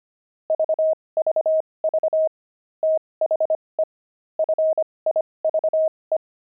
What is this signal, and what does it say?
Morse code, heard as a keyed tone, about 25 wpm, spaced as for 20 wpm, 630 Hz. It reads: VVV THE FIVE
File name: pangram-fs-20-cs-25-jumping